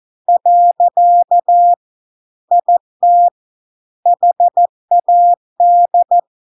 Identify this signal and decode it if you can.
Morse code, heard as a keyed tone, about 14 wpm, 700 Hz. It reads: . IT HAD